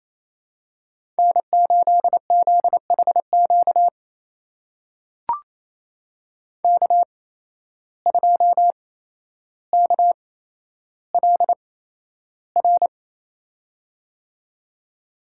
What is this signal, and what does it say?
Morse code, heard as a keyed tone, about 29 wpm, spaced as for 12 wpm, 700 Hz. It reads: N8ZHQ K2KLR
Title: USA Call Signs - 5 Characters - 8x Character Spacing - 28wpm